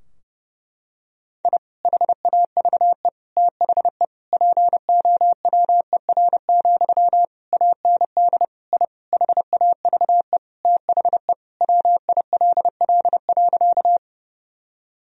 Morse code, heard as a keyed tone, about 30 wpm, 700 Hz. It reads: I HAVE THE POWER, AND I HAVE THE WILL.
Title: Sentences from Top 1000 Words - Review Part 2 - 30wpm